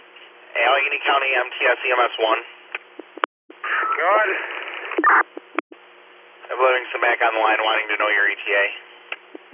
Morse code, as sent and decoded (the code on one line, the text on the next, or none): none